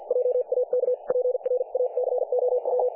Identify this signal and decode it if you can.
Morse code, heard as a keyed tone, about 50 wpm, spaced as for 44 wpm, 500 Hz. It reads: JAK PRACOW
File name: dlazaawansowanych / nagraniazpasma / websdr recording 3564.7kHz